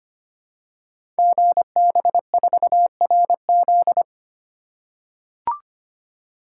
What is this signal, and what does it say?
Morse code, heard as a keyed tone, about 25 wpm, 700 Hz. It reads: GB4RZ
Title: Code Groups - 5 Letter:Numbers - 25wpm